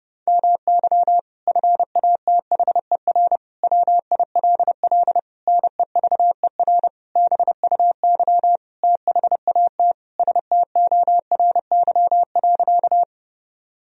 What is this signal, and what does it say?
Morse code, heard as a keyed tone, about 30 wpm, 700 Hz. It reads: MY FATHER WILL NEVER BUY THAT STORY.
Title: Sentences from Top 1000 Words - Review Part 4 - 30wpm